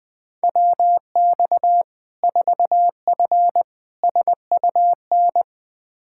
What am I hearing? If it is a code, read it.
Morse code, heard as a keyed tone, about 20 wpm, 700 Hz. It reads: WX 4F SUN
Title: Practice QSO - Exchange 2 - Info - 20wpm